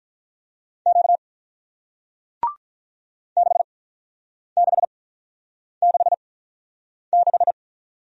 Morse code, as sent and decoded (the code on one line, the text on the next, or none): --.- / -.... -.... -.... -....
Q 6666